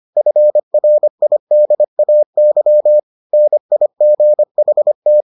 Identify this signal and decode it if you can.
Morse code, heard as a keyed tone, about 25 wpm, 600 Hz. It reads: FRIDAY NIGHT